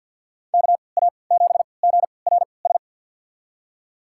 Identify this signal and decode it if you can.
Morse code, heard as a keyed tone, about 53 wpm, spaced as for 29 wpm, 700 Hz. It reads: KA7GRS
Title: USA Call Signs - 6 Characters - 3x Character Spacing - 50wpm